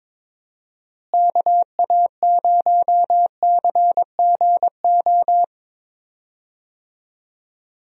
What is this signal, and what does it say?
Morse code, heard as a keyed tone, about 22 wpm, 700 Hz. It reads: KA0CGO
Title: USA Call Signs - 6 Characters - Repeated 3x - 22wpm